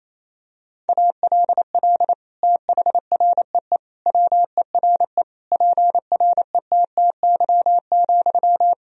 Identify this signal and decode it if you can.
Morse code, heard as a keyed tone, about 27 wpm, 700 Hz. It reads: ALL THREE WERE PRETTY,
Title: Sentences from Top 1000 Words - Review Part 4 - 28wpm